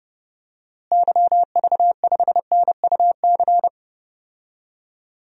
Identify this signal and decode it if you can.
Morse code, heard as a keyed tone, about 32 wpm, 700 Hz. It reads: YV5NUC
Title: DXCC-100 Call Signs - 6 Characters - Repeated 2x - 30wpm